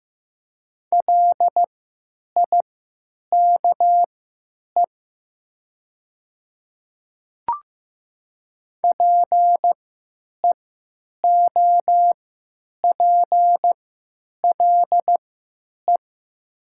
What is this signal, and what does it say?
Morse code, heard as a keyed tone, about 15 wpm, spaced as for 9 wpm, 700 Hz. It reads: LIKE PEOPLE
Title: 3x Character Spacing - Top 100 Words - 15wpm